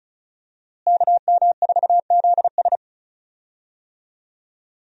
Morse code, heard as a keyed tone, about 34 wpm, 700 Hz. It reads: KM4ZS
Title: USA Call Signs - 5 Characters - Repeated 2x - 35wpm